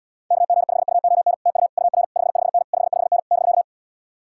Morse code, heard as a keyed tone, about 77 wpm, 700 Hz. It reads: KG5UON TU UR 55N 55N <BK>